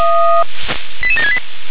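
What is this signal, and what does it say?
Five-tone selective call (ZVEI), heard as quick runs of five tones, tones of about 70 ms. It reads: 88768
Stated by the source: ZVEI five-tone call received by SDR.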